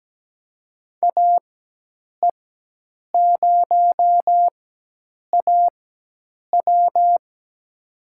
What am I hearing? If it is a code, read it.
Morse code, heard as a keyed tone, about 17 wpm, spaced as for 8 wpm, 700 Hz. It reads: AE0AW
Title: USA Call Signs - 5 Characters - 4x Character Spacing - 17wpm